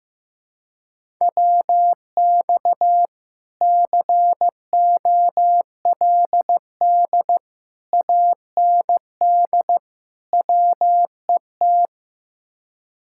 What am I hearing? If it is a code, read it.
Morse code, heard as a keyed tone, about 16 wpm, 700 Hz. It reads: WX COLD AND WET
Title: Speed-Racing Common QSO Phrases Encore - 15wpm